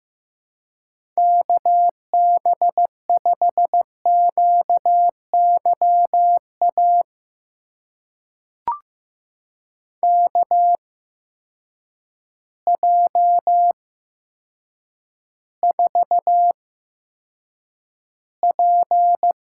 Morse code, heard as a keyed tone, about 15 wpm, spaced as for 8 wpm, 700 Hz. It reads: KB5QYA KJ4P
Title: USA Call Signs - 6 Characters - 8x Character Spacing - 15wpm